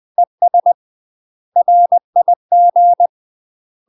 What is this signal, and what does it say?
Morse code, heard as a keyed tone, about 20 wpm, 700 Hz. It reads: ES RIG
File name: UK MASTER QSOs TRAINING / 20WPM / UK MASTER QSOs 0035